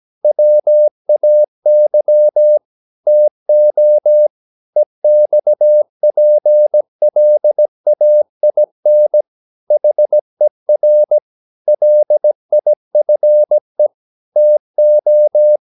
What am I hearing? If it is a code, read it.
Morse code, heard as a keyed tone, about 17 wpm, 600 Hz. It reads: WAY TO EXPLAIN HER LIFE TO
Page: Romanen Never af Ken Follet på engelsk.